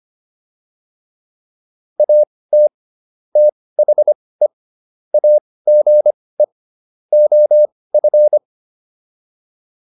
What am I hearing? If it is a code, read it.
Morse code, heard as a keyed tone, about 25 wpm, spaced as for 18 wpm, 600 Hz. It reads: AT THE AGE OF